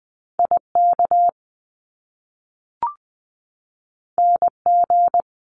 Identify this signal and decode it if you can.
Morse code, heard as a keyed tone, about 20 wpm, 700 Hz. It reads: IK NG
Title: Top Two Letter Combinations in Words 20wpm